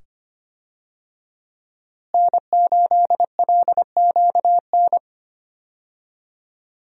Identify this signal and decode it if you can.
Morse code, heard as a keyed tone, about 25 wpm, 700 Hz. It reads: N8LQN
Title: USA Call Signs - 5 Characters - Repeated 2x - 25wpm